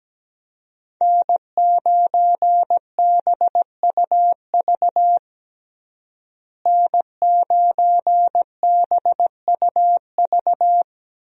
Morse code, heard as a keyed tone, about 17 wpm, 700 Hz. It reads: N9BUV N9BUV
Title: USA Call Signs - 5 Characters - Repeated 3x - 17wpm